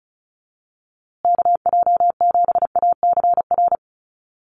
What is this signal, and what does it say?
Morse code, heard as a keyed tone, about 34 wpm, 700 Hz. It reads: KJ7ACR
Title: USA Call Signs Repeated 2x 35wpm